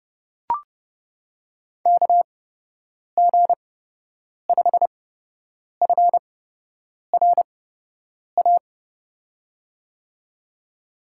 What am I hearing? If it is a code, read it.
Morse code, heard as a keyed tone, about 30 wpm, spaced as for 8 wpm, 700 Hz. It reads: KG5FRA
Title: USA Call Signs - 6 Characters - 8x Character Spacing - 30wpm